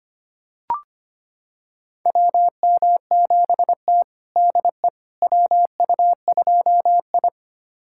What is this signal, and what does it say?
Morse code, heard as a keyed tone, about 25 wpm, 700 Hz. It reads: WM7T DE WU2I
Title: USA Callsign DE Callsign - 4 Characters - 25wpm